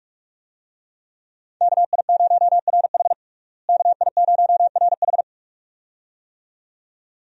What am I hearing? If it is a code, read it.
Morse code, heard as a keyed tone, about 46 wpm, 700 Hz. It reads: KI0RH KI0RH
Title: USA Call Signs - 5 Characters - Repeated 2x - 45wpm